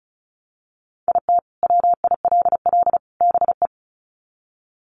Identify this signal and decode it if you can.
Morse code, heard as a keyed tone, about 34 wpm, 700 Hz. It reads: IT WILL BE
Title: Sets of 3 Words 35wpm